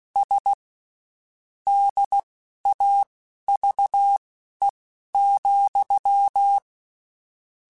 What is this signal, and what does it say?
Morse code, heard as a keyed tone, about 16 wpm, spaced as for 11 wpm, 800 Hz. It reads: S DAVE,
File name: QSOs / QSOs-mp3s / WD0BGP 1@15